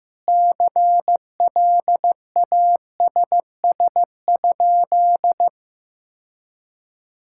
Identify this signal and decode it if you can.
Morse code, heard as a keyed tone, about 15 wpm, 700 Hz. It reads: CLASS?
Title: Sentences from Top 1000 Words - Review Part 6 - 15wpm